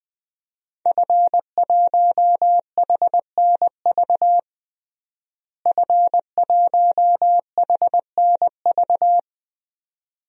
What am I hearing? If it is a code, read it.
Morse code, heard as a keyed tone, about 20 wpm, 700 Hz. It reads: F1HNV F1HNV
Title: DXCC-100 Call Signs - 5 Characters - Repeated 2x - 20wpm